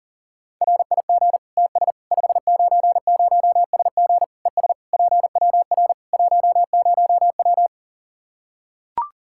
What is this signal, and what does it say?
Morse code, heard as a keyed tone, about 37 wpm, 700 Hz. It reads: RIG TS 590SG ES PWR 10W
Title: Practice QSO - Exchange 2 - Info - 40wpm